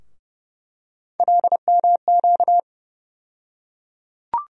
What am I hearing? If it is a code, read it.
Morse code, heard as a keyed tone, about 30 wpm, 700 Hz. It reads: LMQ
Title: USA Call Sign Postfixes - 3 Letters 30wpm